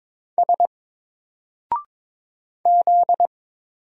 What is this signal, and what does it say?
Morse code, heard as a keyed tone, about 22 wpm, 700 Hz. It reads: S Z
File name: Single-Letter-Number-22wpm